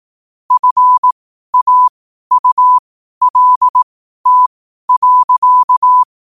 Morse code, heard as a keyed tone, about 18 wpm, spaced as for 12 wpm, 1000 Hz. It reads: FAULT.
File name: qotdfiles / 12wpm